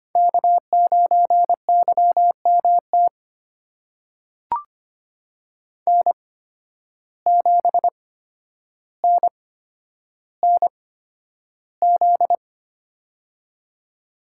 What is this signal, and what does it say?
Morse code, heard as a keyed tone, about 25 wpm, spaced as for 11 wpm, 700 Hz. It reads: K9YMT N7NNZ
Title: USA Call Signs - 5 Characters - 8x Character Spacing - 25wpm